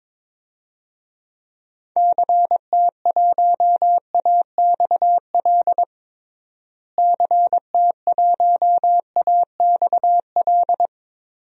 Morse code, heard as a keyed tone, about 22 wpm, 700 Hz. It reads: CT1AXL CT1AXL
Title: DXCC-100 Call Signs - 6 Characters - Repeated 3x - 22wpm